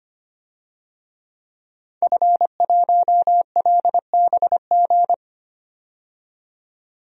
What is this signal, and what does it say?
Morse code, heard as a keyed tone, about 25 wpm, 700 Hz. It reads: F1LBG
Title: DXCC-100 Call Signs - 5 Characters - 25wpm